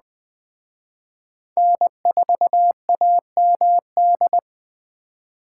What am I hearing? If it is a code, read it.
Morse code, heard as a keyed tone, about 20 wpm, 700 Hz. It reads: N4AMD
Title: Speed Racing - USA Call Signs 17wpm